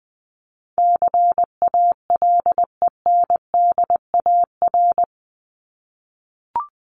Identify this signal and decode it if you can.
Morse code, heard as a keyed tone, about 20 wpm, 700 Hz. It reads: CALENDAR
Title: Top 1501-1750 Words 20wpm